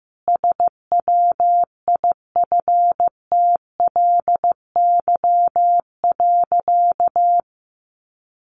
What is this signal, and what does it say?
Morse code, heard as a keyed tone, about 15 wpm, 700 Hz. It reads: SWIFTLY.
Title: A Sound Of Thunder 15wpm